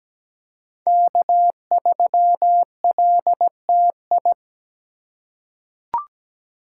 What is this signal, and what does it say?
Morse code, heard as a keyed tone, about 17 wpm, 700 Hz. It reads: K3LTI
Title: USA Call Signs - 5 Characters - 3x Character Spacing - 17wpm